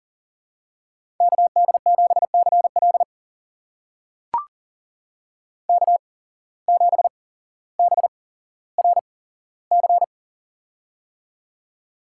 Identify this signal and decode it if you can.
Morse code, heard as a keyed tone, about 40 wpm, spaced as for 17 wpm, 700 Hz. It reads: KD7CL K7BRC
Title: USA Call Signs - 5 Characters - 8x Character Spacing - 40wpm